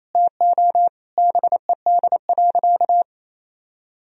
Morse code, heard as a keyed tone, about 28 wpm, 700 Hz. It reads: TO BED.
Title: Sentences from Top 1000 Words - Review Part 3 - 28wpm